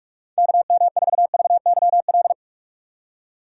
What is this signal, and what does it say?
Morse code, heard as a keyed tone, about 46 wpm, 700 Hz. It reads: KM4VYL
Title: USA Call Signs - 6 Characters - 3x Character Spacing - 45wpm